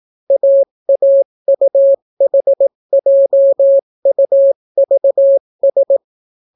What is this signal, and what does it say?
Morse code, heard as a keyed tone, about 18 wpm, 550 Hz. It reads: AAUHJUVS